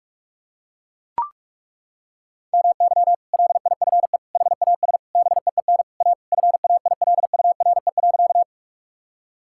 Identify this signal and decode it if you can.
Morse code, heard as a keyed tone, about 41 wpm, 700 Hz. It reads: MY LIFE HAS BEEN A FAILURE.
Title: Sentences from Top 1000 Words - Review Part 6 - 45wpm